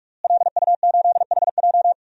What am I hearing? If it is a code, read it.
Morse code, heard as a keyed tone, about 47 wpm, 700 Hz. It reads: LU8HJ
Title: DXCC-100 Call Signs - 5 Characters - Repeated 3x - 45wpm